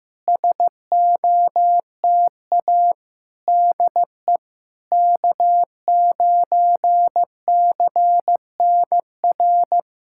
Morse code, heard as a keyed tone, about 15 wpm, 690 Hz. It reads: SOTA DE K9CNR
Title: SOTA QSO Practice 15wpm